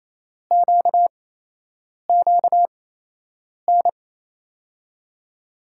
Morse code, heard as a keyed tone, about 28 wpm, spaced as for 8 wpm, 700 Hz. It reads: QQN
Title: USA Call Signs - 8x Character Spacing - 28wpm